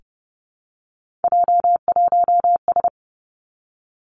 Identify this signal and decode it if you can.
Morse code, heard as a keyed tone, about 30 wpm, 700 Hz. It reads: J1S
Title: Three Challenging Letter-Number Combinations 30wpm